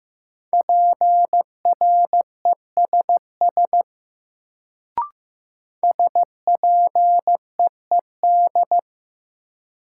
Morse code, heard as a keyed tone, about 15 wpm, 700 Hz. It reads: PRESS SPEED
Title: Sentences from Top 800 Words - New Word Drill - 15wpm